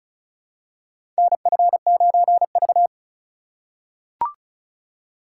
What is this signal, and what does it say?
Morse code, heard as a keyed tone, about 36 wpm, 700 Hz. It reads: NF9V
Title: USA Call Signs - 4 Characters - 3x Character Spacing - 35wpm